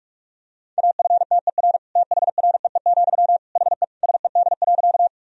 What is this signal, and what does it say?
Morse code, heard as a keyed tone, about 46 wpm, 700 Hz. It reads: AFTER THREE, HE SED.
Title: Sentences from Top 200 Words and Abbreviations 45wpm